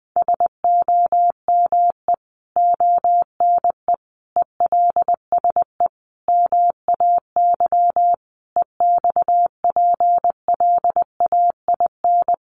Morse code, heard as a keyed tone, about 20 wpm, 700 Hz. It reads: SOME ONE ELSE MAY EXPLAIN
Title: Sentences from Top 1100 Words 20wpm